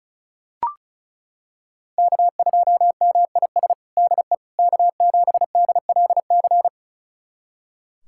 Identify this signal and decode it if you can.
Morse code, heard as a keyed tone, about 34 wpm, 700 Hz. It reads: K2MIS DE K7DLC
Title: USA Callsign DE Callsign - 5 Characters - 35wpm